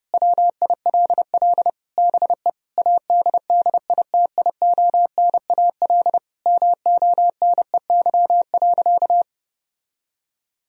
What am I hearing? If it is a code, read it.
Morse code, heard as a keyed tone, about 30 wpm, 700 Hz. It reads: WILL BE ADDITIONAL MONEY.
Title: Sentences from Top 1000 Words - Review Part 4 - 30wpm